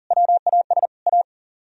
Morse code, heard as a keyed tone, about 48 wpm, 700 Hz. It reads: WAS A
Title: Speed-Racing - Sets of 5 Words 40wpm